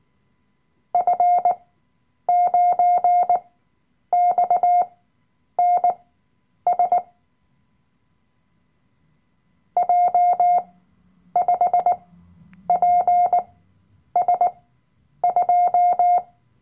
Morse code, heard as a keyed tone, about 19 wpm, spaced as for 9 wpm, 710 Hz. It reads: F9XNS J5PS2
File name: Workshop Dateien / CW-Online-Kurs / Lektion 007